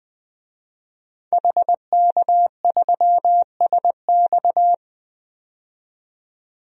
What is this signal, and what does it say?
Morse code, heard as a keyed tone, about 20 wpm, 700 Hz. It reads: HK3SX
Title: DXCC-100 Call Signs - 5 Characters - 20wpm